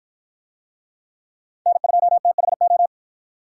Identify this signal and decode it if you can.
Morse code, heard as a keyed tone, about 53 wpm, 700 Hz. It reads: N2THO